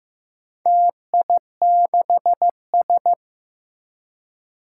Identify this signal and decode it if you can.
Morse code, heard as a keyed tone, about 15 wpm, 700 Hz. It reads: TI6S
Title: Four Challenging Letter-Number Combinations 15wpm